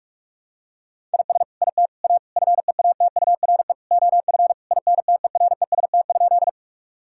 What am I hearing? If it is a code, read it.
Morse code, heard as a keyed tone, about 46 wpm, 700 Hz. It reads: IS IT A FEATURE OF INTEREST?
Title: Sentences from Top 1000 Words - Review Part 5 - 45wpm